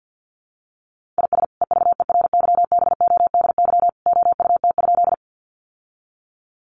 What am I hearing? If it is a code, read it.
Morse code, heard as a keyed tone, about 48 wpm, 700 Hz. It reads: IS EVERYBODY OUT?
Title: Sentences from Top 1400 Words 50wpm